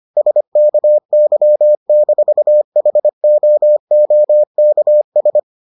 Morse code, heard as a keyed tone, about 25 wpm, 600 Hz. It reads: SKY-HOOKS